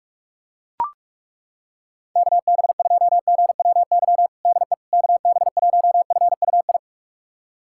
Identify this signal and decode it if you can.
Morse code, heard as a keyed tone, about 41 wpm, 700 Hz. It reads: KB2GWY DE KB1FUI
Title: USA Callsign DE Callsign - 6 Characters - 45wpm